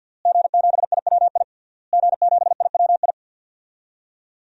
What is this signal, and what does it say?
Morse code, heard as a keyed tone, about 47 wpm, 700 Hz. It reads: G7IWI G7IWI
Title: DXCC-100 Call Signs - 5 Characters - Repeated 3x - 50wpm